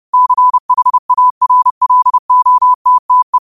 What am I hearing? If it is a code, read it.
Morse code, heard as a keyed tone, about 30 wpm, 1000 Hz. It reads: CHARLOTTE
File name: qotdfiles / 30wpm